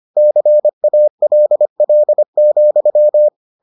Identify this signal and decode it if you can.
Morse code, heard as a keyed tone, about 25 wpm, 600 Hz. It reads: CALL,